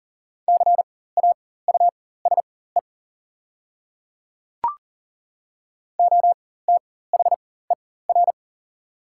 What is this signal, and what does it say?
Morse code, heard as a keyed tone, about 40 wpm, spaced as for 19 wpm, 700 Hz. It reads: CAUSE OTHER
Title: 4x Character Spacing - Top 200 Words - 40wpm